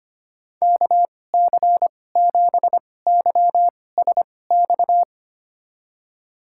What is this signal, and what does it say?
Morse code, heard as a keyed tone, about 25 wpm, spaced as for 18 wpm, 700 Hz. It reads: KC7YSX
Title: USA Call Signs - 2x Character Spacing - 25wpm